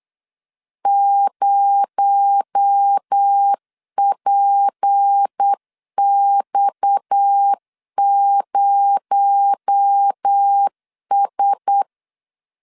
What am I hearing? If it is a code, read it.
Morse code, heard as a keyed tone, about 8 wpm, 800 Hz. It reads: 0PX0S
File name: CW / DET / 1 9 1500C S A 8wpm 5637